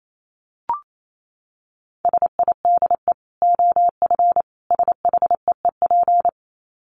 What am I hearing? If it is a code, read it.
Morse code, heard as a keyed tone, about 28 wpm, 700 Hz. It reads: SIDE OF SHEEP
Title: Dit Challenge 28wpm